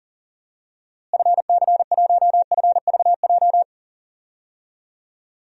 Morse code, heard as a keyed tone, about 37 wpm, 700 Hz. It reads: FC1FVJ
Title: DXCC-100 Call Signs - 6 Characters - Repeated 2x - 40wpm